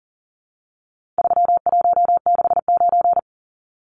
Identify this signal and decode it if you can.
Morse code, heard as a keyed tone, about 40 wpm, 700 Hz. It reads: 3169
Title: Numbers - 4-Digit 40wpm